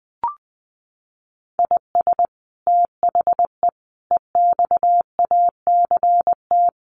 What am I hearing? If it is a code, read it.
Morse code, heard as a keyed tone, about 20 wpm, 700 Hz. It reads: IS THE EXACT
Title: Sentences from Top 1300 Words 20wpm